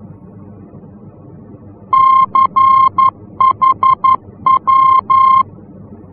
Morse code, heard as a keyed tone, about 11 wpm, 1060 Hz. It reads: CHW
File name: france / mx / CHW-F-VOR